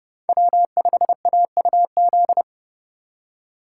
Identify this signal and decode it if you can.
Morse code, heard as a keyed tone, about 27 wpm, 700 Hz. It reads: W5AUZ